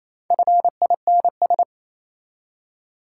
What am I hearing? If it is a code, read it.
Morse code, heard as a keyed tone, about 27 wpm, 690 Hz. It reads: FINS
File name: Easy Words - Speed-Racing with Hints - 22wpm